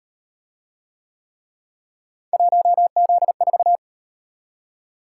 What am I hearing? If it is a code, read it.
Morse code, heard as a keyed tone, about 38 wpm, 700 Hz. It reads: 1Z4